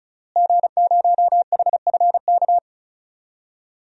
Morse code, heard as a keyed tone, about 36 wpm, 700 Hz. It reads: G0HFK
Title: DXCC-100 Call Signs - 5 Characters - Repeated 3x - 35wpm